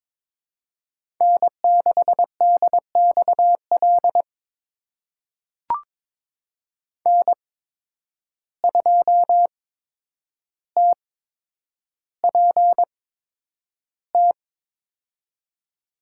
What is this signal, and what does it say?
Morse code, heard as a keyed tone, about 22 wpm, spaced as for 9 wpm, 700 Hz. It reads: N6DXL N2TPT
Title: USA Call Signs - 5 Characters - 8x Character Spacing - 22wpm